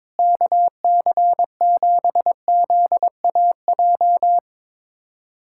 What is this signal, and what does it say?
Morse code, heard as a keyed tone, about 22 wpm, 700 Hz. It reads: KC7ZAJ
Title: USA Callsign DE Callsign - 6 Characters - 22wpm